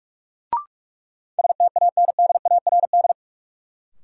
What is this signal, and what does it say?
Morse code, heard as a keyed tone, about 45 wpm, 700 Hz. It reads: STANDARD
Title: Sentences from Top 1200 Words - New Word Drill Alt - 45wpm